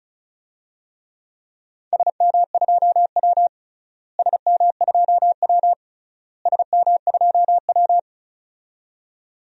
Morse code, heard as a keyed tone, about 36 wpm, 700 Hz. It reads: SM2W SM2W SM2W
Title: DXCC-100 Call Signs - 4 Characters - Repeated 3x - 35wpm